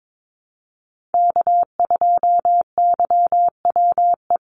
Morse code, heard as a keyed tone, about 22 wpm, 700 Hz. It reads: K2YWE
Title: Call Signs Repeated 2x - 22wpm